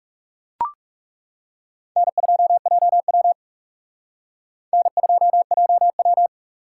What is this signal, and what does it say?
Morse code, heard as a keyed tone, about 40 wpm, 700 Hz. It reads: N2JW N2JW